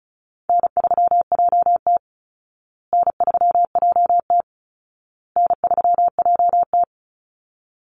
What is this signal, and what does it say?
Morse code, heard as a keyed tone, about 34 wpm, 700 Hz. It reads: N3JT N3JT N3JT
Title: Call Signs Repeated 2x - 35wpm